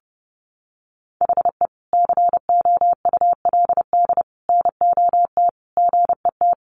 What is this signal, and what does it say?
Morse code, heard as a keyed tone, about 28 wpm, 700 Hz. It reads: HE COULD NOT GET
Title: Sentences from Top 1100 Words 30wpm